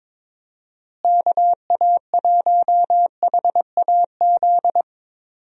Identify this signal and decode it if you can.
Morse code, heard as a keyed tone, about 22 wpm, 700 Hz. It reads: KA1HAZ
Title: USA Call Signs - 6 Characters - Repeated 2x - 22wpm